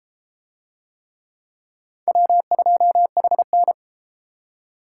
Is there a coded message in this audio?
Yes: Morse code from 2.1 to 3.7 s, heard as a keyed tone, about 30 wpm, 700 Hz. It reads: W2HN